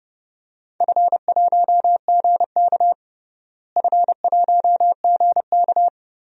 Morse code, heard as a keyed tone, about 29 wpm, 700 Hz. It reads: F1GK F1GK
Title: DXCC-100 Call Signs - 4 Characters - Repeated 3x - 30wpm